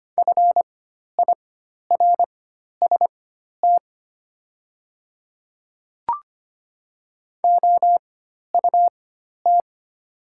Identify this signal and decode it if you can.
Morse code, heard as a keyed tone, about 25 wpm, spaced as for 12 wpm, 700 Hz. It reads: FIRST OUT
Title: 4x Character Spacing - Top 100 Words - 25wpm